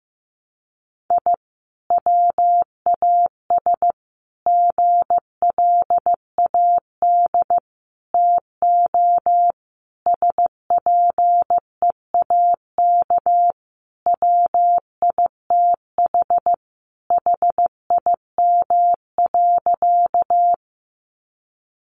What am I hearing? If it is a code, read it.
Morse code, heard as a keyed tone, about 15 wpm, 700 Hz. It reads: I WAS GLAD TO SPEAK WITH HIM.
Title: Sentences from Top 1100 Words 15wpm